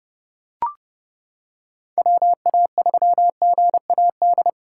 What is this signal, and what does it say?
Morse code, heard as a keyed tone, about 30 wpm, 700 Hz. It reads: WA3GAD